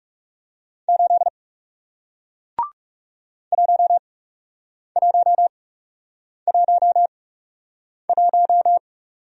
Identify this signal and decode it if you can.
Morse code, heard as a keyed tone, about 39 wpm, spaced as for 8 wpm, 700 Hz. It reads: 8 1111